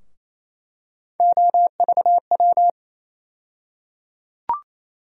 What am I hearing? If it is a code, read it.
Morse code, heard as a keyed tone, about 28 wpm, 700 Hz. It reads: OVW
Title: USA Call Sign Postfixes - 3 Letters 28wpm